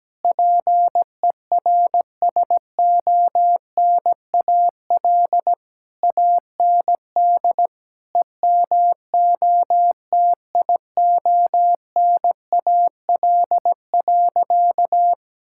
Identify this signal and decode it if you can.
Morse code, heard as a keyed tone, about 17 wpm, 700 Hz. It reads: PERSONAL AND EMOTIONAL.
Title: Sentences from Top 1000 Words - Review Part 5 - 17wpm